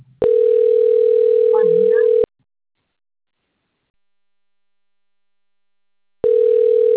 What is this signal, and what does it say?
Ringback tone (North America), 440 + 480 Hz, with speech.
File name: rg-1600-unknown- - - .91554